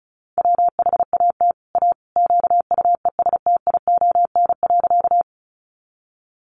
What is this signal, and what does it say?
Morse code, heard as a keyed tone, about 34 wpm, 700 Hz. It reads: WHAT A QUESTION.
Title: Sentences from Top 400 Words 35wpm